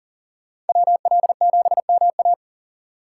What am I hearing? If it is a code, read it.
Morse code, heard as a keyed tone, about 38 wpm, 700 Hz. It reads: WL7MA
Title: USA Call Signs - 5 Characters - 6x Character Spacing - 40wpm